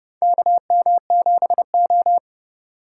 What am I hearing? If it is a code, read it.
Morse code, heard as a keyed tone, about 32 wpm, 700 Hz. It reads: KM7O
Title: USA Call Signs - 4 Characters - 3x Character Spacing - 30wpm